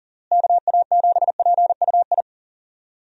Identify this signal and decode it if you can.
Morse code, heard as a keyed tone, about 40 wpm, 700 Hz. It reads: KA7PUI
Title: USA Call Signs - 6 Characters - Repeated 3x - 40wpm